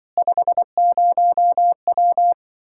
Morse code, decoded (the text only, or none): 50W